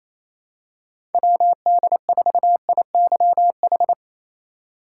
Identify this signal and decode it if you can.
Morse code, heard as a keyed tone, about 28 wpm, 700 Hz. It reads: WD4IYH
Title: USA Call Signs - 6 Characters - 3x Character Spacing - 28wpm